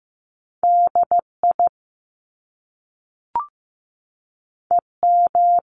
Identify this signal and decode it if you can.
Morse code, heard as a keyed tone, about 15 wpm, 700 Hz. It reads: DI EM
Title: Top Two Letter Combinations in Words 15wpm